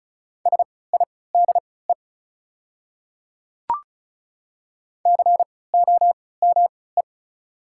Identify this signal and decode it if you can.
Morse code, heard as a keyed tone, about 36 wpm, spaced as for 20 wpm, 700 Hz. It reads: SIDE COME
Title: 3x Character Spacing - Top 100 Words - 35wpm